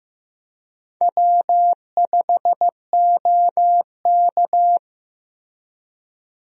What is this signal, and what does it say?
Morse code, heard as a keyed tone, about 15 wpm, 700 Hz. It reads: W5OK
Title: USA Call Signs - 4 Characters - Repeated 2x - 15wpm